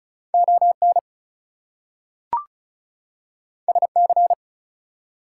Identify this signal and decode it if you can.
Morse code, heard as a keyed tone, about 36 wpm, 700 Hz. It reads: ON SC
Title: US States and Canadian Provinces 35wpm